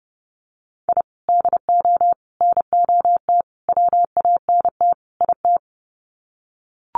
Morse code, heard as a keyed tone, about 30 wpm, 700 Hz. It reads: I DO NOT WANT IT
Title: Sets of 5 Words Encore2 30wpm